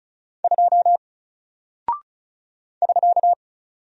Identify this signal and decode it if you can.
Morse code, heard as a keyed tone, about 36 wpm, 700 Hz. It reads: 2 <SK>
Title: Single Character (Letter:Number:Punctuation:Prosign) Alt 35wpm